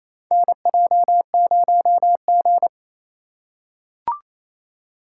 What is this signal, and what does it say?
Morse code, heard as a keyed tone, about 28 wpm, 700 Hz. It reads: NJ0G
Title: USA Call Signs - 4x Character Spacing - 28wpm